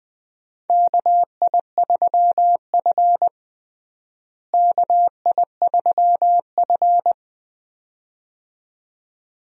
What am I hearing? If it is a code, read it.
Morse code, heard as a keyed tone, about 20 wpm, 700 Hz. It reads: KI3F KI3F
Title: USA Call Signs - 4 Characters - Repeated 2x - 20wpm